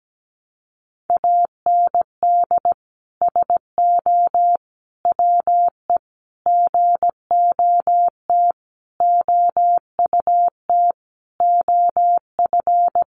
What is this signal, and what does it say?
Morse code, heard as a keyed tone, about 17 wpm, 700 Hz. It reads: AND SO WE GOT OUT OF
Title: Sentences from Top 1500 Words 17wpm